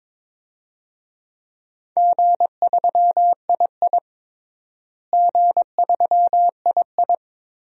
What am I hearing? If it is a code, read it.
Morse code, heard as a keyed tone, about 22 wpm, 700 Hz. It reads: G3II G3II
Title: DXCC-100 Call Signs - 4 Characters - Repeated 2x - 22wpm